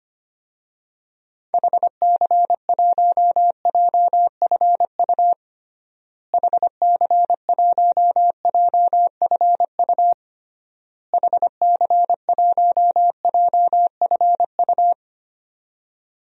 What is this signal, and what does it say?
Morse code, heard as a keyed tone, about 25 wpm, 700 Hz. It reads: HC1JFU HC1JFU HC1JFU
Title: DXCC-100 Call Signs - 6 Characters - Repeated 3x - 25wpm